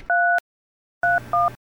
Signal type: DTMF